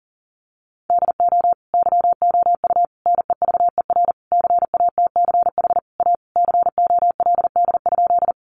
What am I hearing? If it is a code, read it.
Morse code, heard as a keyed tone, about 40 wpm, 700 Hz. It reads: DO YOU NEVER CATCH A COLD?
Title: Sentences from Top 1500 Words 40wpm